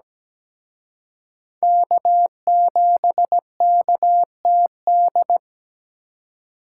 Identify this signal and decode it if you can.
Morse code, heard as a keyed tone, about 20 wpm, 700 Hz. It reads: K7KTD
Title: Speed Racing - USA 1x3 Call Signs 17wpm